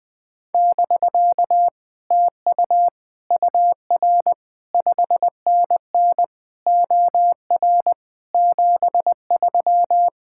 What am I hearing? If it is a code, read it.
Morse code, heard as a keyed tone, about 20 wpm, 700 Hz. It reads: <BK> TU UR 5NN OR 73
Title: Speed-Racing Common QSO Phrases Encore - 20wpm